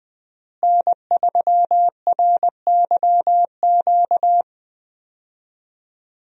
Morse code, heard as a keyed tone, about 20 wpm, 700 Hz. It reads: N3RYQ
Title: USA Call Signs - 5 Characters - Repeated 2x - 20wpm